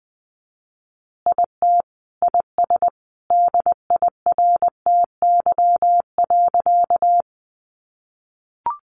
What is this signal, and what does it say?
Morse code, heard as a keyed tone, about 20 wpm, 700 Hz. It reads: IT IS DIRTY.
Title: Additional Easy Phrases 20wpm